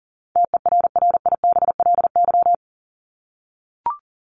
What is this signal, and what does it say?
Morse code, heard as a keyed tone, about 40 wpm, 700 Hz. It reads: TERRIBLY
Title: Top 1751-2000 Words 40wpm